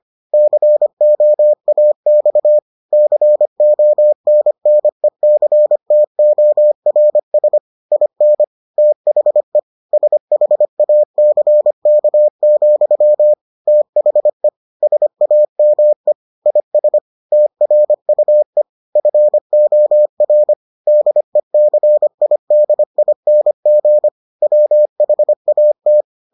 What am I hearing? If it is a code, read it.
Morse code, heard as a keyed tone, about 25 wpm, 600 Hz. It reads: COAX CONNECTORS IN THE SHACK, THE SAME IS TRUE FOR DECIDING WHAT